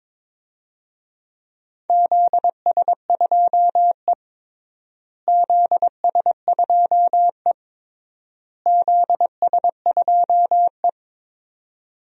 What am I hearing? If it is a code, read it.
Morse code, heard as a keyed tone, about 22 wpm, 700 Hz. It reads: ZS2E ZS2E ZS2E
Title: DXCC-100 Call Signs - 4 Characters - Repeated 3x - 22wpm